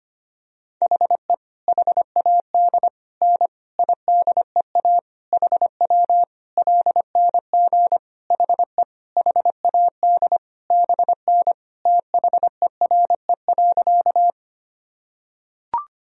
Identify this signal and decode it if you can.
Morse code, heard as a keyed tone, about 25 wpm, 700 Hz. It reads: HE HAD N IDEA HW LNG HE HAD BN THERE.
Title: Sentences from Top 300 Words and Abbreviations 25wpm